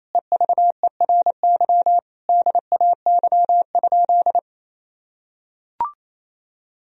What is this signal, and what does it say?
Morse code, heard as a keyed tone, about 28 wpm, 700 Hz. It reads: EVERY DAY?
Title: Sentences from Top 1000 Words - Review Part 4 - 28wpm